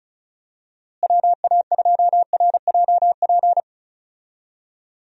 Morse code, heard as a keyed tone, about 34 wpm, 700 Hz. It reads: WA2RJP
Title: USA Call Signs - 6 Characters - Repeated 2x - 35wpm